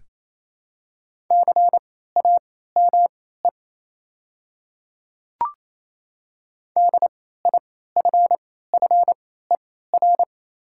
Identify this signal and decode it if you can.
Morse code, heard as a keyed tone, about 28 wpm, spaced as for 16 wpm, 700 Hz. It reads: CAME DIFFER
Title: 3x Character Spacing - Top 200 Words - 28wpm